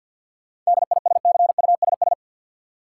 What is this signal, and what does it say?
Morse code, heard as a keyed tone, about 53 wpm, 700 Hz. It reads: DISCUSS
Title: Sentences from Top 1000 Words - New Word Drill - 50wpm